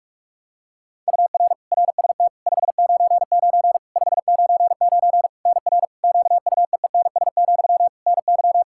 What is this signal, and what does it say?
Morse code, heard as a keyed tone, about 46 wpm, 700 Hz. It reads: UR RST 599 599 NR QUEENS, NY
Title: K1USN SST QSO Practice 45wpm